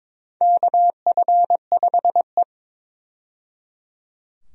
Morse code, heard as a keyed tone, about 22 wpm, 700 Hz. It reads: KF5E
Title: USA Call Signs - 4 Characters - 22wpm